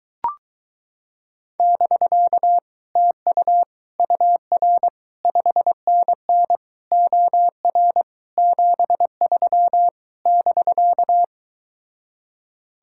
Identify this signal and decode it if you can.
Morse code, heard as a keyed tone, about 23 wpm, 700 Hz. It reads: <BK> TU UR 5NN OR 73 <BK>